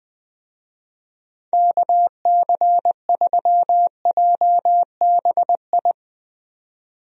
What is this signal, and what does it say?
Morse code, heard as a keyed tone, about 20 wpm, 700 Hz. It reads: KC3JBI
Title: USA Call Signs - 6 Characters - 20wpm